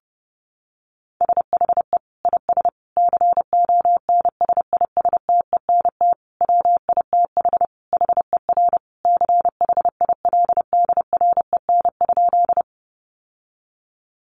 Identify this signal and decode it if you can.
Morse code, heard as a keyed tone, about 30 wpm, 700 Hz. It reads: SHE IS CONSISTENT WITH HER CHILDREN?
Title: Sentences from Top 1500 Words 30wpm